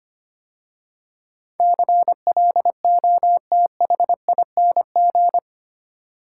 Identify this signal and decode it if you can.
Morse code, heard as a keyed tone, about 25 wpm, 700 Hz. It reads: CLOTHING
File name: 8 Letter Words - First 3 Letters Given - 25wpm